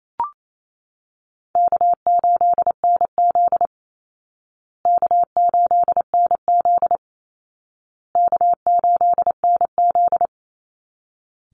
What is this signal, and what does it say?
Morse code, heard as a keyed tone, about 28 wpm, 700 Hz. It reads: K8NZ K8NZ K8NZ
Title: Easy Call Signs Repeated 3x - 28wpm